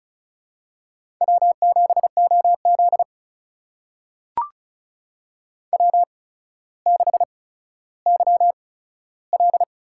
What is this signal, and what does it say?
Morse code, heard as a keyed tone, about 36 wpm, spaced as for 15 wpm, 700 Hz. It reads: W7OZ W6YL
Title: USA Call Signs - 4 Characters - 8x Character Spacing - 35wpm